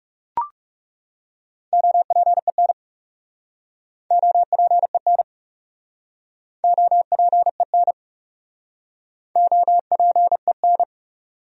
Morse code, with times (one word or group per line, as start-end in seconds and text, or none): 1.7-2.7 OPEN
4.1-5.2 OPEN
6.6-7.9 OPEN
9.4-10.8 OPEN